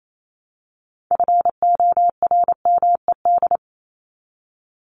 Morse code, heard as a keyed tone, about 28 wpm, 700 Hz. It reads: FORMED
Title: ED Suffix - Top 100 Words with Base Words 28wpm